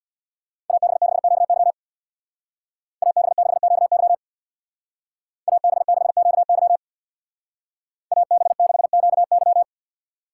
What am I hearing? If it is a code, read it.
Morse code, heard as a keyed tone, about 64 wpm, 700 Hz. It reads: AB6QY AB6QY AB6QY AB6QY